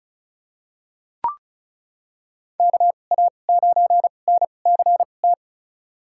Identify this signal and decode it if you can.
Morse code, heard as a keyed tone, about 34 wpm, spaced as for 25 wpm, 690 Hz. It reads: KA9NCT
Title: USA Call Signs - 6 Characters - 2x Character Spacing - 35wpm